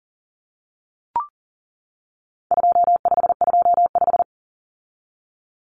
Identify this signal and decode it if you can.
Morse code, heard as a keyed tone, about 40 wpm, 700 Hz. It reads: 2525
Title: Common Cut Numbers - 4-Digit 40wpm